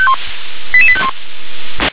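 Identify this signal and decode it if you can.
Five-tone selective call (ZVEI), heard as quick runs of five tones, tones of about 70 ms. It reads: 88851
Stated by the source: ZVEI five-tone selective call received on SDR.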